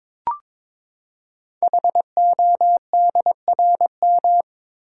Morse code, heard as a keyed tone, about 23 wpm, 700 Hz. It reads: HODRM
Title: Code Groups - 5 Letters - 22wpm